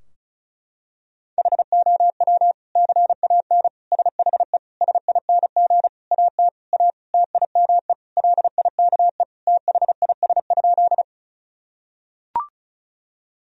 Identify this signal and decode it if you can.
Morse code, heard as a keyed tone, about 36 wpm, 700 Hz. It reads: HOW CAN SHE SING AT A TIME LIKE THIS?
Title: Sentences from Top 1000 Words - Review Part 2 - 35wpm